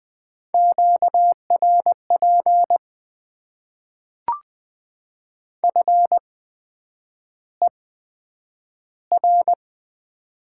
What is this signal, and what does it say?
Morse code, heard as a keyed tone, about 20 wpm, spaced as for 9 wpm, 700 Hz. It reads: QRP FER
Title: Top 90 QSO Elements - 8x Character Spacing - 20wpm